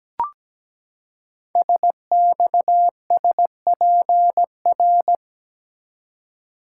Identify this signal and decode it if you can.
Morse code, heard as a keyed tone, about 17 wpm, 700 Hz. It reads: SXSPR
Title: Code Groups - 5 Letters - 17wpm